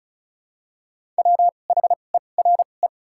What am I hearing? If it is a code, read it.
Morse code, heard as a keyed tone, about 34 wpm, spaced as for 25 wpm, 700 Hz. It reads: WHERE
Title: 2x Character Spacing - Top 200 Words - 35wpm